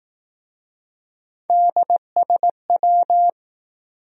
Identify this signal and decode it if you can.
Morse code, heard as a keyed tone, about 20 wpm, 700 Hz. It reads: DSW